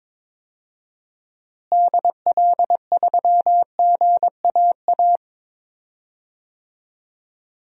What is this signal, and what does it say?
Morse code, heard as a keyed tone, about 21 wpm, 700 Hz. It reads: DL3GAA
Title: DXCC-100 Call Signs - 6 Characters - Repeated 3x - 22wpm